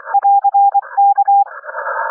Morse code, heard as a keyed tone, about 24 wpm, 800 Hz. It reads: <AR>K
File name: n6tr / N6TR-7 1827 le 14-11-2009 07h48-at-F6DDR